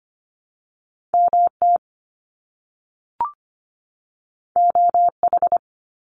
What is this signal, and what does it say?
Morse code, heard as a keyed tone, about 25 wpm, 700 Hz. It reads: MT OH
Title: US States 25wpm